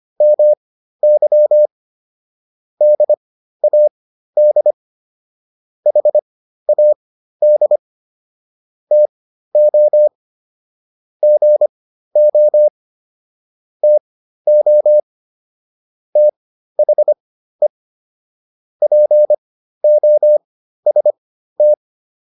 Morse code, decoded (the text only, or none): MY DAD HAD TO GO TO THE POST